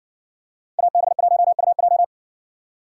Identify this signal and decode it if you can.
Morse code, heard as a keyed tone, about 60 wpm, 700 Hz. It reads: AB1UJ